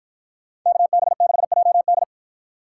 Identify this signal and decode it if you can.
Morse code, heard as a keyed tone, about 53 wpm, 700 Hz. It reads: KB6JD